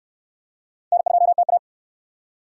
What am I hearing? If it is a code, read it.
Morse code, heard as a keyed tone, about 66 wpm, 700 Hz. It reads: N2TA